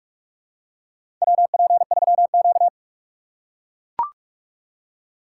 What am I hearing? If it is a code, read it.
Morse code, heard as a keyed tone, about 44 wpm, 690 Hz. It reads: WP3Q
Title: USA Call Signs - 4 Characters - 4x Character Spacing - 45wpm